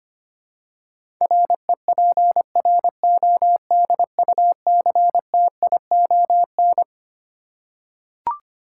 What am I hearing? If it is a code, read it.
Morse code, heard as a keyed tone, about 25 wpm, 700 Hz. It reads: REPRODUCTION
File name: Twelve Letter Words - 25wpm